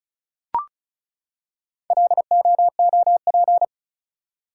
Morse code, heard as a keyed tone, about 36 wpm, 700 Hz. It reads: LOOP
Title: Top 180 QSO Elements 35wpm